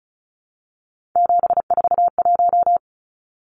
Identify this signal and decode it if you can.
Morse code, heard as a keyed tone, about 34 wpm, 700 Hz. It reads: 741
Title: Numbers - 3-Digit 35wpm